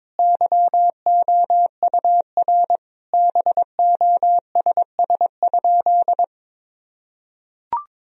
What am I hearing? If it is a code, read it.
Morse code, heard as a keyed tone, about 22 wpm, 700 Hz. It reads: YOUR BOSS?
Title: Sentences from Top 1000 Words - Review Part 4 - 22wpm